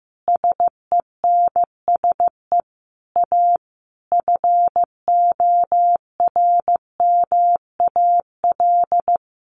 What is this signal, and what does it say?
Morse code, heard as a keyed tone, about 15 wpm, 700 Hz. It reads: SENSE A FORMAL
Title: Sentences from Top 1300 Words 15wpm